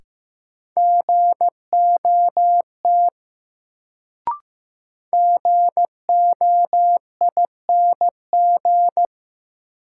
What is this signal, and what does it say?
Morse code, heard as a keyed tone, about 16 wpm, 700 Hz. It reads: GOT GOING
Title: Sets of 2 Words - New Word Drill - 15wpm